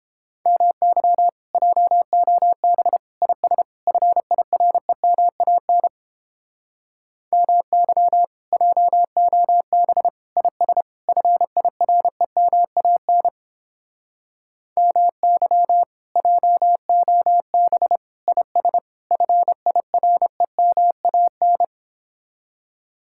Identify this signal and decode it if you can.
Morse code, heard as a keyed tone, about 29 wpm, 700 Hz. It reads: MY JOB IS FIREMAN MY JOB IS FIREMAN MY JOB IS FIREMAN